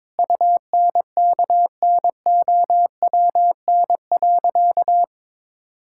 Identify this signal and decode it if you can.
Morse code, heard as a keyed tone, about 21 wpm, 700 Hz. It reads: UNKNOWN.
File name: practice / 22wpm / Sentences with 4-Word Phrases - 22wpm